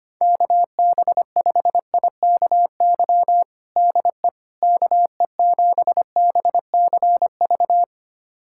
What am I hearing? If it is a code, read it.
Morse code, heard as a keyed tone, about 25 wpm, 700 Hz. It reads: KB5IKY DE KE7BCV
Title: USA Callsign DE Callsign - 6 Characters - 25wpm